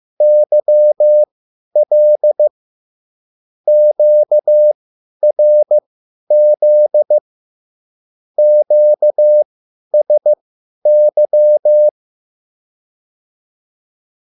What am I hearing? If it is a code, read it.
Morse code, heard as a keyed tone, about 15 wpm, spaced as for 10 wpm, 600 Hz. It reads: YL QRZ QSY